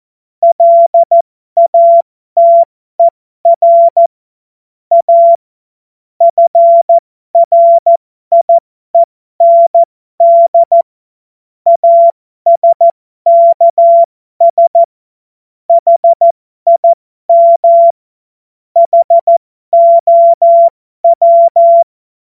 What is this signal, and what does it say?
Morse code, heard as a keyed tone, about 14 wpm, spaced as for 12 wpm, 680 Hz. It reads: LATER A FRIEND ASKS HIM HOW
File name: Morsekurs / Witze